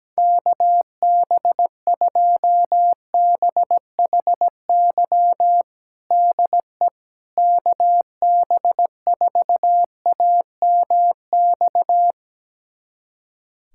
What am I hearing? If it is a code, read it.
Morse code, heard as a keyed tone, about 17 wpm, 700 Hz. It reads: KB2BHY DE KB4AMX
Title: USA Callsign DE Callsign - 6 Characters - 17wpm